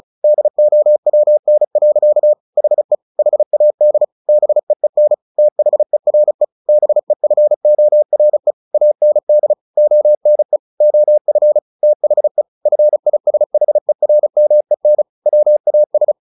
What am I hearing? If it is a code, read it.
Morse code, heard as a keyed tone, about 34 wpm, 600 Hz. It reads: DOWN. HE HAD BEEN THERE BEFORE AND ONE OF THE FISHERMEN WAS
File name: the old man and the sea 2 35wpm